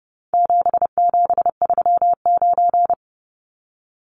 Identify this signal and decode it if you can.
Morse code, heard as a keyed tone, about 30 wpm, 700 Hz. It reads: 7739
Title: Numbers - 4-Digit 30wpm